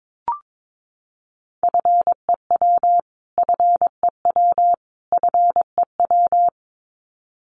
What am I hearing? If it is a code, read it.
Morse code, heard as a keyed tone, about 22 wpm, 700 Hz. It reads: FEW FEW FEW
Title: Top 300 Words Repeated 22wpm